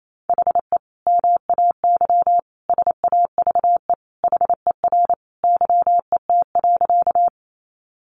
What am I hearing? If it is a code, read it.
Morse code, heard as a keyed tone, about 28 wpm, 700 Hz. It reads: HE MAY SAVE HER YET.
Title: Sentences from Top 1100 Words 28wpm